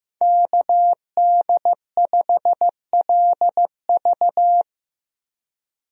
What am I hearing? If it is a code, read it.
Morse code, heard as a keyed tone, about 15 wpm, 700 Hz. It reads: KD5LV
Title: USA Call Signs - 5 Characters - 15wpm